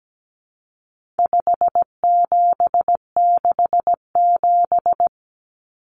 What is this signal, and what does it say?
Morse code, heard as a keyed tone, about 17 wpm, 700 Hz. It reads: 5767
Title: Common Cut Numbers - 4-Digit 17wpm